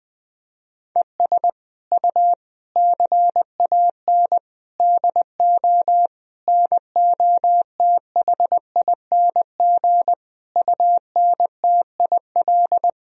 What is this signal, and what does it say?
Morse code, heard as a keyed tone, about 20 wpm, 700 Hz. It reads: ES U CAN DO NOTHING UNTIL
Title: Sentences from Top 500 Words and Abbreviations 20wpm